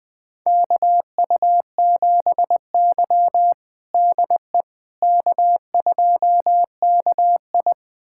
Morse code, heard as a keyed tone, about 20 wpm, 700 Hz. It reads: KU7Y DE K2KI
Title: USA Callsign DE Callsign - 4 Characters - 20wpm